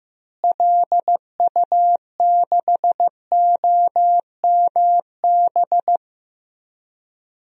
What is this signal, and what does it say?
Morse code, heard as a keyed tone, about 15 wpm, 700 Hz. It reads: LU6OMB
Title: DXCC-100 Call Signs - 6 Characters - Repeated 3x - 15wpm